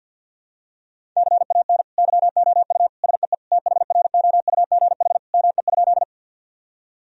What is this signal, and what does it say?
Morse code, heard as a keyed tone, about 49 wpm, 700 Hz. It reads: CAN YOU SEE THROUGH ME?
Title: Sentences from Top 1000 Words - Review Part 1 - 50wpm